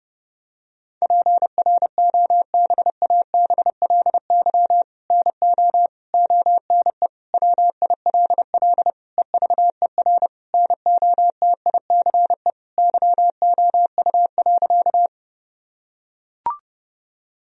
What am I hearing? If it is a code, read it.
Morse code, heard as a keyed tone, about 30 wpm, 700 Hz. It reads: PROBABLY NO ONE WILL EVER NOTICE YOU.
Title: Sentences from Top 1000 Words - Review Part 4 - 30wpm